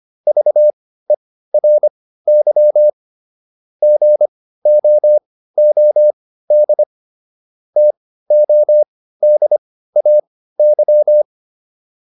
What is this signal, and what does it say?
Morse code, heard as a keyed tone, about 25 wpm, spaced as for 15 wpm, 600 Hz. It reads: VERY GOOD TODAY